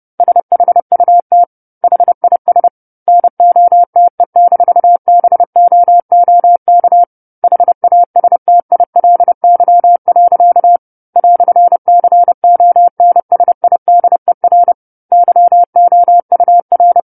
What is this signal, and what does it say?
Morse code, heard as a keyed tone, about 30 wpm, 700 Hz. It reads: SHUT HIS NOTE-BOOK HASTILY. "CONSIDER YOUR
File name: 10-Chapter-10 30 WPM